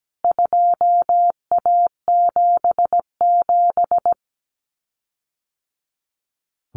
Morse code, heard as a keyed tone, about 17 wpm, 700 Hz. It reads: 2A77
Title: CWT Exchange 17wpm